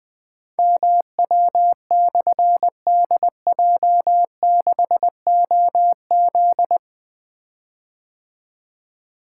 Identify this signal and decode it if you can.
Morse code, heard as a keyed tone, about 20 wpm, 700 Hz. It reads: MW/DJ6OZ
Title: Difficult Call Signs 20wpm